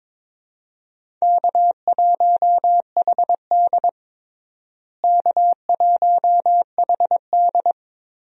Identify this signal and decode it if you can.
Morse code, heard as a keyed tone, about 22 wpm, 700 Hz. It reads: K1HD K1HD
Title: USA Call Signs - 4 Characters - Repeated 3x - 22wpm